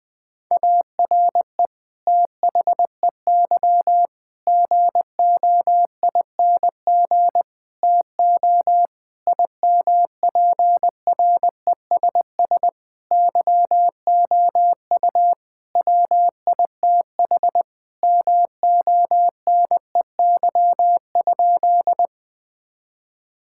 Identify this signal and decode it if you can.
Morse code, heard as a keyed tone, about 21 wpm, 700 Hz. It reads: ARE THEY GOING TO IMPRESS YOU WITH MONEY?
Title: Sentences from Top 1600 Words 20wpm